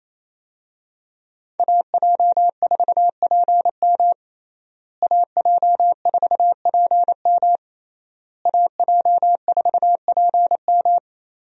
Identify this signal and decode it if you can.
Morse code, heard as a keyed tone, about 28 wpm, 700 Hz. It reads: AJ4PM AJ4PM AJ4PM
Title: USA Call Signs - 5 Characters - Repeated 3x - 28wpm